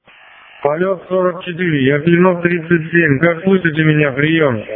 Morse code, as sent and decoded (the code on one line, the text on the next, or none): none